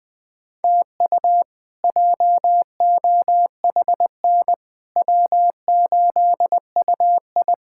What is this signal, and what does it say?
Morse code, heard as a keyed tone, about 20 wpm, 700 Hz. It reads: TU JOHN W8UI
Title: K1USN SST QSO Practice - 20wpm